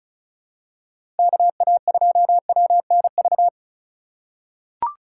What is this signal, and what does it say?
Morse code, heard as a keyed tone, about 36 wpm, 700 Hz. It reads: KA2WNV
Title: USA Call Signs - 3x Character Spacing - 35wpm